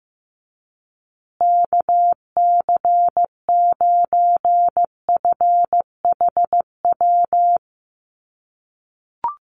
Morse code, heard as a keyed tone, about 15 wpm, 700 Hz. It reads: KC9FHW
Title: USA Call Signs Repeated 3x 15wpm